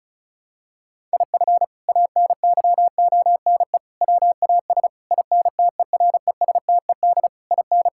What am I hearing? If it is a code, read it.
Morse code, heard as a keyed tone, about 34 wpm, 700 Hz. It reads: IF ANYONE WAS INTERESTED IN
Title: Sentences from Top 1600 Words 35wpm